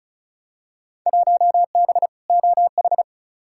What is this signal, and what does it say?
Morse code, heard as a keyed tone, about 36 wpm, 700 Hz. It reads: 1B OH
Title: Field Day Exchanges - 35wpm